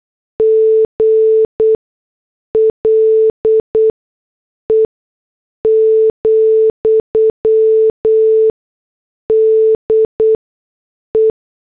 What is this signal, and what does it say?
Morse code, heard as a keyed tone, about 8 wpm, spaced as for 6 wpm, 440 Hz. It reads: GLE, DE
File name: morse / headlines